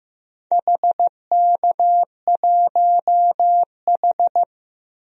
Morse code, heard as a keyed tone, about 15 wpm, 700 Hz. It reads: HK1H
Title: DXCC-100 Call Signs - 4 Characters - 15wpm